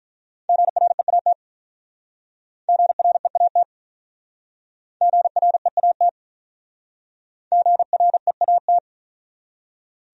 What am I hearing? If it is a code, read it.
Morse code, heard as a keyed tone, about 41 wpm, 700 Hz. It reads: GREAT GREAT GREAT GREAT
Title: Speed-Racing Top 200 Words 35wpm